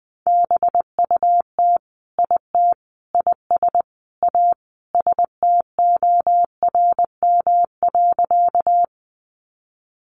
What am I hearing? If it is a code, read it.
Morse code, heard as a keyed tone, about 20 wpm, 700 Hz. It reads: BUT IT IS A STORM.
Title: Sentences from Top 1500 Words 20wpm